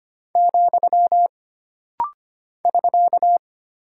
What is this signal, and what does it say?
Morse code, heard as a keyed tone, about 25 wpm, 700 Hz. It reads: , <SK>
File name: Single Punctuation-Prosign - Rapid-Fire - 25wpm